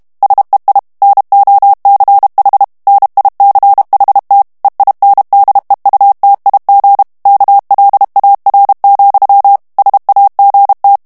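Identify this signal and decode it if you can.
Morse code, heard as a keyed tone, about 32 wpm, 800 Hz. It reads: SEI NOCH NICHT EINDEUTIG KLAR, SAGT
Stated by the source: German plain-text Morse reading practice at 30 words per minute.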